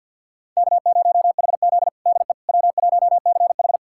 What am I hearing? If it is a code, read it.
Morse code, heard as a keyed tone, about 51 wpm, 700 Hz. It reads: K0HZ DE W1CH
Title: USA Callsign DE Callsign - 4 Characters - 50wpm